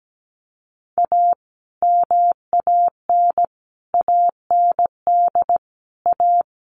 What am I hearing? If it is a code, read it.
Morse code, heard as a keyed tone, about 17 wpm, 700 Hz. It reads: A MAN AND A
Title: Sets of 5 Words Encore 17wpm